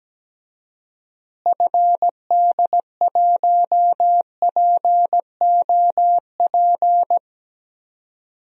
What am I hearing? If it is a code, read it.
Morse code, heard as a keyed tone, about 17 wpm, 700 Hz. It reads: FD1POP
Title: DXCC-100 Call Signs - 6 Characters - Repeated 2x - 17wpm